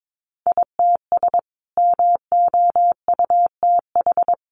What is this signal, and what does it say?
Morse code, heard as a keyed tone, about 22 wpm, 700 Hz. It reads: ITS MOUTH
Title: A Sound Of Thunder 22wpm